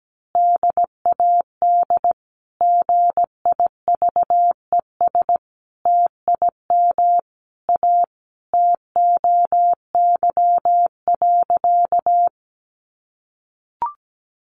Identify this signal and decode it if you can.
Morse code, heard as a keyed tone, about 17 wpm, 700 Hz. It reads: DAD GIVES TIM A TOY.
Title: Easy Phrases 17wpm